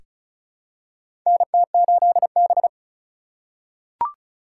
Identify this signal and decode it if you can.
Morse code, heard as a keyed tone, about 36 wpm, 700 Hz. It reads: NT8B
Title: USA Call Signs - 4 Characters - 3x Character Spacing - 35wpm